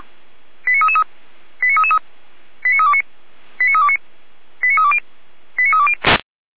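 Five-tone selective call (ZVEI), heard as quick runs of five tones, tones of about 70 ms. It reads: 89333 89333 89329 89329 89320 89320
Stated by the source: ZVEI five-tone call received by SDR, fire brigade Feuerwehr Orlamünde.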